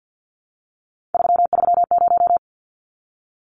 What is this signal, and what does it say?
Morse code, heard as a keyed tone, about 49 wpm, 700 Hz. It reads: 330
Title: Numbers - 3-Digit 50wpm